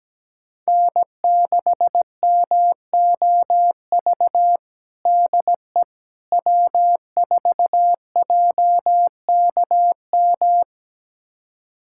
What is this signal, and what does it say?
Morse code, heard as a keyed tone, about 17 wpm, 700 Hz. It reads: N6MOV DE W4JKM
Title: USA Callsign DE Callsign - 5 Characters - 17wpm